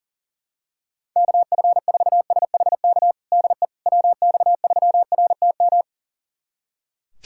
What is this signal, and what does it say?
Morse code, heard as a keyed tone, about 40 wpm, 700 Hz. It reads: KF4SHK DE WX3RTM
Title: USA Callsign DE Callsign - 6 Characters - 40wpm